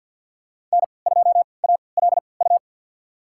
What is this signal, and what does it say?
Morse code, heard as a keyed tone, about 52 wpm, spaced as for 29 wpm, 700 Hz. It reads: N2ALU
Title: USA Call Signs - 3x Character Spacing - 50wpm